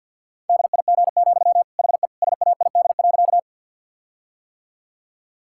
Morse code, heard as a keyed tone, about 50 wpm, 700 Hz. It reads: DIG, HE SAID.
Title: Sentences from Top 1600 Words 50wpm